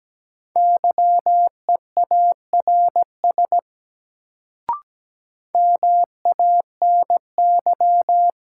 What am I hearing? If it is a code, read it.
Morse code, heard as a keyed tone, about 17 wpm, 700 Hz. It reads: YEARS MANY
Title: Sets of 2 Words - New Word Drill - 17wpm